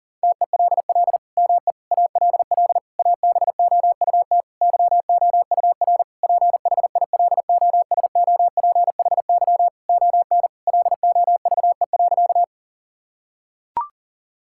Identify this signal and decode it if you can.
Morse code, heard as a keyed tone, about 40 wpm, 690 Hz. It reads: TELL ME ALL ABOUT YOUR PHILOSOPHY ON LOVE.
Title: Sentences from Top 1000 Words - Review Part 6 - 40wpm